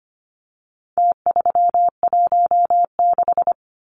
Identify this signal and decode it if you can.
Morse code, heard as a keyed tone, about 25 wpm, 700 Hz. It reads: T316
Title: Numbers - Contest Style - 4-Digit 25wpm